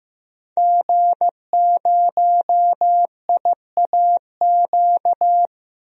Morse code, heard as a keyed tone, about 15 wpm, 700 Hz. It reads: G0IAQ
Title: DXCC-100 Call Signs - 5 Characters - Repeated 3x - 15wpm